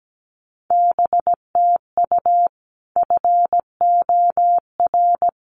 Morse code, heard as a keyed tone, about 17 wpm, 700 Hz. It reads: BTU FOR
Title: Common QSO Phrases 17wpm